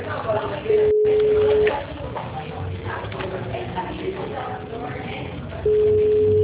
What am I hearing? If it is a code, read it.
Ringback tone (Europe, 425 Hz), with speech.
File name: out- -2008- - - .61438